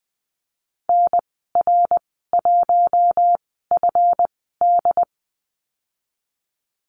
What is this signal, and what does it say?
Morse code, heard as a keyed tone, about 20 wpm, spaced as for 15 wpm, 700 Hz. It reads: NR1FD
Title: 147 Lesson 24g - Letter y - Call Signs 20wpm